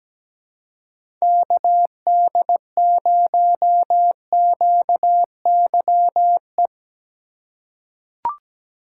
Morse code, heard as a keyed tone, about 17 wpm, 700 Hz. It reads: KD0QYE
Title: USA Call Signs - 6 Characters - 2x Character Spacing - 17wpm